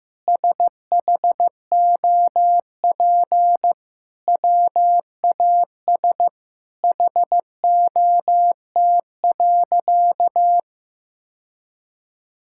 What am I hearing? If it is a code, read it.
Morse code, heard as a keyed tone, about 15 wpm, 700 Hz. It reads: SHOP WAS HOT.
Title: Sentences from Top 1600 Words 15wpm